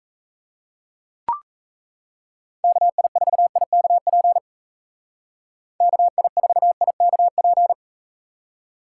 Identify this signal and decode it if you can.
Morse code, heard as a keyed tone, about 39 wpm, 700 Hz. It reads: KI4IKP KI4IKP